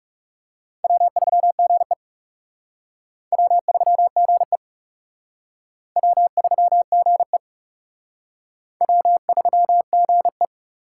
Morse code, heard as a keyed tone, about 36 wpm, 700 Hz. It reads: W3GE W3GE W3GE W3GE
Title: Speed Racing - USA 1x2 Call Signs 30wpm